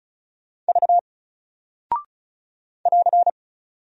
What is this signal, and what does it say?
Morse code, heard as a keyed tone, about 36 wpm, 700 Hz. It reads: V <AR>
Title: Single Character (Letter:Number:Punctuation:Prosign) Alt 35wpm